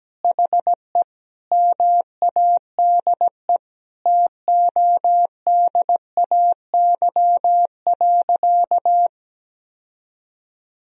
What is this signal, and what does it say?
Morse code, heard as a keyed tone, about 17 wpm, 700 Hz. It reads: HE MADE TODAY.
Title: Sentences from Top 1000 Words - Review Part 2 - 17wpm